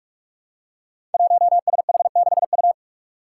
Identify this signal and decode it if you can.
Morse code, heard as a keyed tone, about 42 wpm, 700 Hz. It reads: 1SH6U
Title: Five Challenging Letter-Number Combinations 45wpm